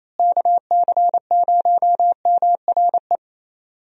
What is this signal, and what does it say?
Morse code, heard as a keyed tone, about 28 wpm, 700 Hz. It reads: KC0MRE
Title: USA Call Signs - 6 Characters - Repeated 2x - 28wpm